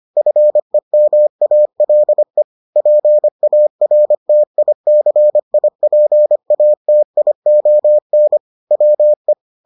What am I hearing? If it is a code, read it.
Morse code, heard as a keyed tone, about 25 wpm, 600 Hz. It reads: FEMALE PARTICIPATION WE